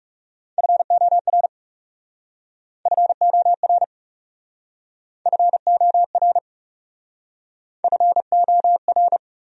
Morse code, heard as a keyed tone, about 39 wpm, 700 Hz. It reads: FOR FOR FOR FOR